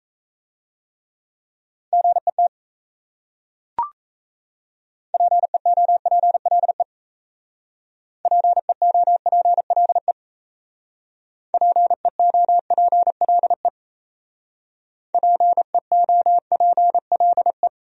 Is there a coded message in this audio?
Yes: Morse code from 1.9 to 17.7 s, heard as a keyed tone, about 36 wpm, 700 Hz. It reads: GET PEOPLE PEOPLE PEOPLE PEOPLE